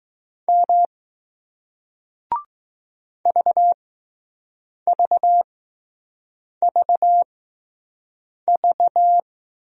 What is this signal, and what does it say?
Morse code, heard as a keyed tone, about 20 wpm, 700 Hz. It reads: M V V V V